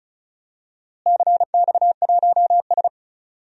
Morse code, heard as a keyed tone, about 36 wpm, 700 Hz. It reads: CX1S
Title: DXCC-100 Call Signs - 4 Characters - Repeated 3x - 35wpm